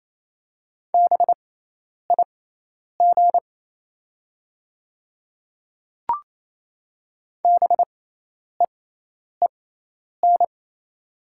Morse code, heard as a keyed tone, about 28 wpm, spaced as for 10 wpm, 700 Hz. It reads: BIG BEEN
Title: 6x Character Spacing - Top 200 Words - 28wpm